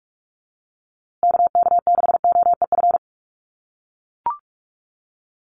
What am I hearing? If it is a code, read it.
Morse code, heard as a keyed tone, about 44 wpm, 690 Hz. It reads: KK6OEF
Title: USA Call Signs Repeated 3x 45wpm